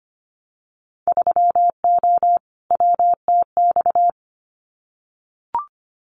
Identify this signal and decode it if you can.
Morse code, heard as a keyed tone, about 25 wpm, 700 Hz. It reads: 3O WTX
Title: Winter Field Day Exchanges Encore 25wpm